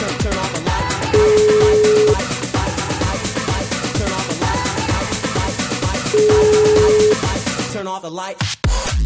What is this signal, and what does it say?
Ringback tone (Europe, 425 Hz), with noise.